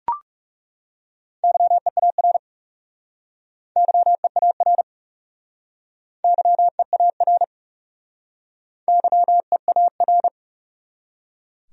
Morse code, heard as a keyed tone, about 41 wpm, 700 Hz. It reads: YEAR YEAR YEAR YEAR